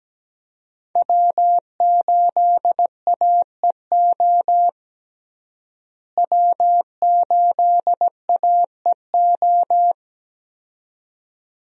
Morse code, heard as a keyed tone, about 17 wpm, 700 Hz. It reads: W8AEO W8AEO
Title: USA Call Signs - 5 Characters - Repeated 2x - 17wpm